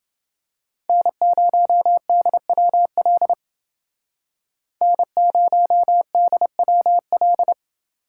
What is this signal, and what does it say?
Morse code, heard as a keyed tone, about 28 wpm, 700 Hz. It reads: N0DWL N0DWL